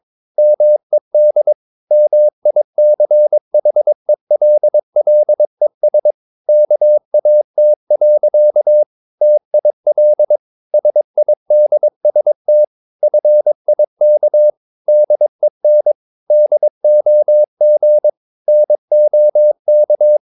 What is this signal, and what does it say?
Morse code, heard as a keyed tone, about 22 wpm, 600 Hz. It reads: MED MICHELLES KAT. TIL SIDST FIK DEN DOG NOK
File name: Ferie QRS 22wpm